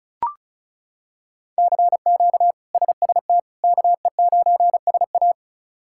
Morse code, heard as a keyed tone, about 36 wpm, 700 Hz. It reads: CQ SST KE9SA
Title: K1USN SST QSO Practice - 35wpm